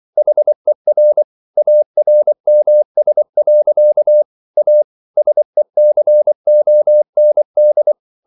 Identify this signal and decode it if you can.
Morse code, heard as a keyed tone, about 24 wpm, 600 Hz. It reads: HER ARMS. A SECOND